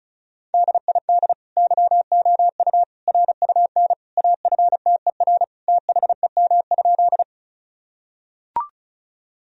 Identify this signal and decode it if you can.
Morse code, heard as a keyed tone, about 36 wpm, 700 Hz. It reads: DID YOU RUN AFTER THEM?
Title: Sentences from Top 1000 Words - Review Part 1 - 35wpm